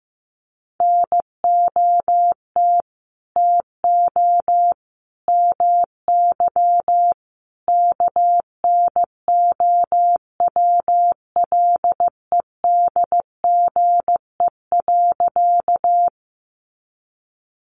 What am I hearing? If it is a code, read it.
Morse code, heard as a keyed tone, about 15 wpm, 700 Hz. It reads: NOT TO MY KNOWLEDGE.
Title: Sentences from Top 600 Words 15wpm